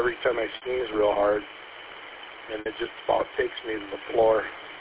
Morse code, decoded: none

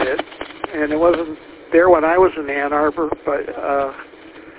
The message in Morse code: none

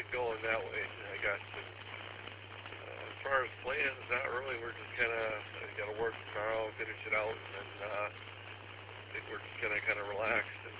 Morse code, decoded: none

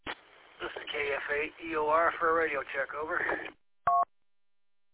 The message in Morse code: none

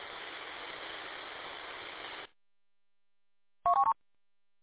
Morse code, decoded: none

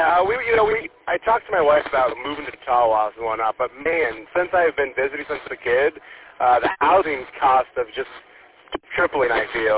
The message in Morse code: none

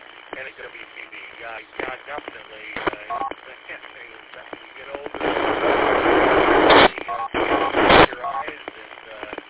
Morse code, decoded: none

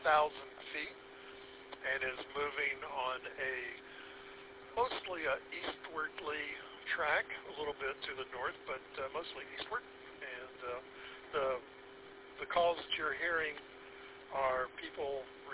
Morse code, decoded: none

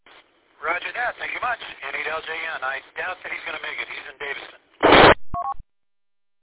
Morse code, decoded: none

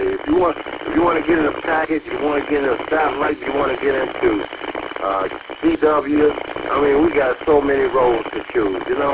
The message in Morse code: none

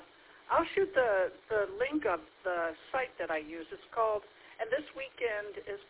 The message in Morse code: none

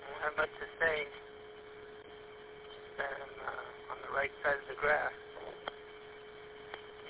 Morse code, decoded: none

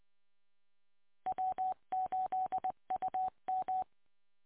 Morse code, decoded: W8UM

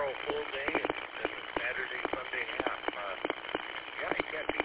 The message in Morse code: none

